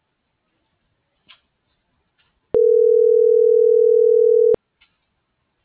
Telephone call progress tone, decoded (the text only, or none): ringback tone